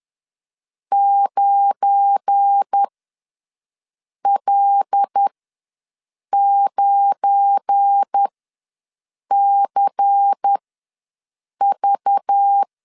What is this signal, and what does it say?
Morse code, heard as a keyed tone, about 10 wpm, spaced as for 6 wpm, 800 Hz. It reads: 9L9CV
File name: LEZIONI / 7 GF9Z / 7 GF9 ws 10wpm 4034